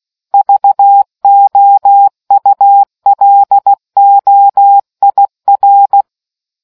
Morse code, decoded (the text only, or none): VOULOIR